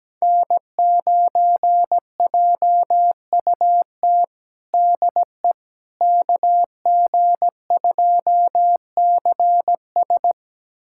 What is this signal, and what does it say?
Morse code, heard as a keyed tone, about 17 wpm, 700 Hz. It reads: N9JUT DE KG2CS
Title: USA Callsign DE Callsign - 5 Characters - 17wpm